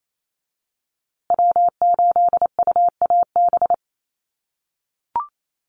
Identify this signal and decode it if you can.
Morse code, heard as a keyed tone, about 28 wpm, 700 Hz. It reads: W8UAB
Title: USA Call Signs Repeated 2x 28wpm